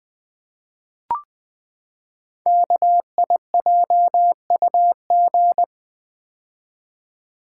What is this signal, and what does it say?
Morse code, heard as a keyed tone, about 20 wpm, 700 Hz. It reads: KIJUG
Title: Code Groups - 5 Letters - 20wpm